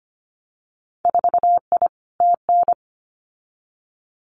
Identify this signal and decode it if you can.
Morse code, heard as a keyed tone, about 25 wpm, 700 Hz. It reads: 4I TN
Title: Winter Field Day Exchanges 25wpm